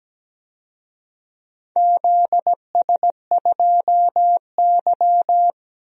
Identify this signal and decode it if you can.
Morse code, heard as a keyed tone, about 17 wpm, 700 Hz. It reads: ZS2Y
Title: DXCC-100 Call Signs - 4 Characters - Repeated 3x - 17wpm